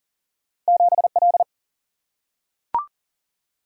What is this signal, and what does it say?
Morse code, heard as a keyed tone, about 40 wpm, 700 Hz. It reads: 7 L
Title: Single-Letter Single-Number 40wpm